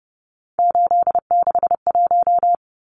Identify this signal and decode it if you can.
Morse code, heard as a keyed tone, about 30 wpm, 700 Hz. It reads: 861
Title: Numbers - 3-Digit 30wpm